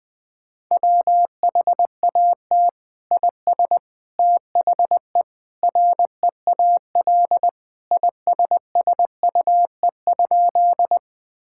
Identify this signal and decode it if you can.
Morse code, heard as a keyed tone, about 20 wpm, 700 Hz. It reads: WHAT IS THE REAL ISSUE?
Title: Sentences from Top 1000 Words - Review Part 6 - 20wpm